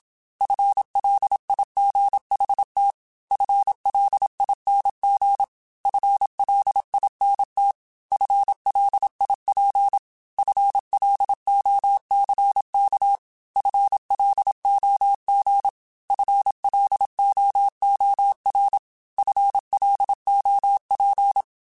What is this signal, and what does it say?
Morse code, heard as a keyed tone, about 27 wpm, 800 Hz. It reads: FLIGHT FLING FLINT FLIP FLOCK FLOG FLOOR FLOP
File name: FL Runs / FL Runs mp3s / FL Runs @25